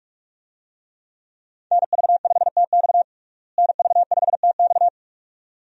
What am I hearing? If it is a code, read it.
Morse code, heard as a keyed tone, about 46 wpm, 700 Hz. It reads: NV5TX NV5TX
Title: USA Call Signs - 5 Characters - Repeated 2x - 45wpm